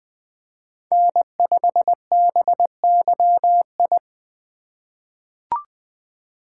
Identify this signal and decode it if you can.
Morse code, heard as a keyed tone, about 20 wpm, 700 Hz. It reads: N5BYI
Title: USA Call Signs - 3x Character Spacing - 20wpm